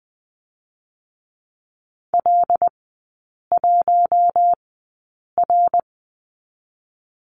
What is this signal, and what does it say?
Morse code, heard as a keyed tone, about 20 wpm, spaced as for 8 wpm, 690 Hz. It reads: L1R
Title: 087 Lesson 15e - Number 5 - Three Characters 20wpm